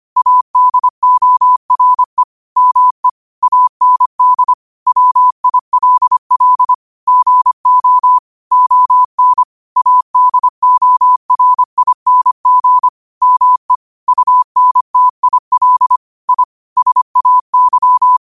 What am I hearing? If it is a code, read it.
Morse code, heard as a keyed tone, about 25 wpm, 1000 Hz. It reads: ADORE ME AND WILL GO ON ADORING ME UNTIL I SAY
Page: Quotes for Fri, 15 Aug 2025 in Morse Code at 25 words per minute.